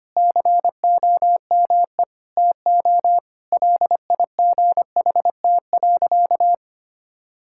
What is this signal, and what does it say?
Morse code, heard as a keyed tone, about 25 wpm, 700 Hz. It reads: COME TO LIGHT.
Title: Sentences from Top 1000 Words - Review Part 5 - 25wpm